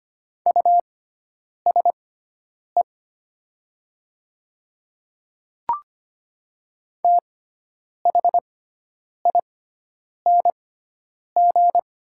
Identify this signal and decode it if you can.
Morse code, heard as a keyed tone, about 25 wpm, spaced as for 9 wpm, 690 Hz. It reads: USE THING
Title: 6x Character Spacing - Top 200 Words - 25wpm